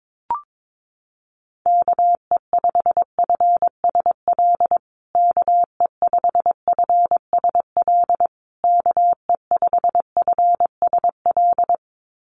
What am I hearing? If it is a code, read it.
Morse code, heard as a keyed tone, about 22 wpm, 700 Hz. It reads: KE5FSL KE5FSL KE5FSL
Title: USA Call Signs Repeated 3x 22wpm